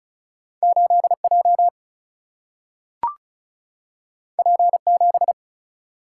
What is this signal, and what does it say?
Morse code, heard as a keyed tone, about 36 wpm, 700 Hz. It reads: 8 J P 7
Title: Single-Letter Single-Number 35wpm